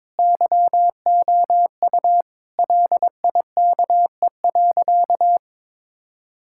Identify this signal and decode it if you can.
Morse code, heard as a keyed tone, about 22 wpm, 700 Hz. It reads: YOU LIKE.
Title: Sentences from Top 1600 Words 22wpm